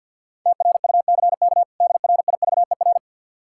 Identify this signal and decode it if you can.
Morse code, heard as a keyed tone, about 51 wpm, 700 Hz. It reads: TRUCK DRIVER
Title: Practice QSO - Exchange 3 - More Info - 50wpm